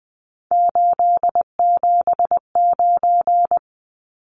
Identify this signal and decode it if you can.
Morse code, heard as a keyed tone, about 20 wpm, 700 Hz. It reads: 879
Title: Numbers - 3-Digit 20wpm